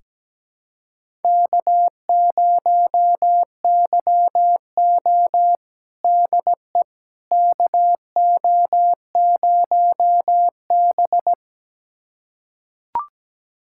Morse code, heard as a keyed tone, about 17 wpm, 700 Hz. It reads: K0YO DE KO0B
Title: USA Callsign DE Callsign - 4 Characters - 17wpm